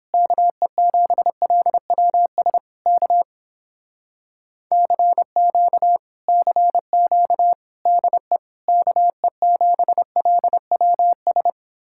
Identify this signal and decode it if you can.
Morse code, heard as a keyed tone, about 29 wpm, 700 Hz. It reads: KE7LWS K CQ CQ DE KE7LWS